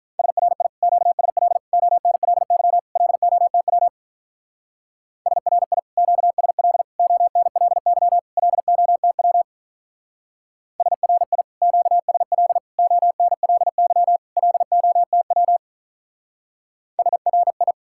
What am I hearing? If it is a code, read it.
Morse code, heard as a keyed tone, about 44 wpm, 700 Hz. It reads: SRI QSL ONLY LOTW SRI QSL ONLY LOTW SRI QSL ONLY LOTW SRI